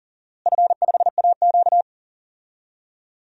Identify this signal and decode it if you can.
Morse code, heard as a keyed tone, about 40 wpm, 700 Hz. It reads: F5AQ
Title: DXCC-100 Call Signs - 4 Characters - Repeated 2x - 40wpm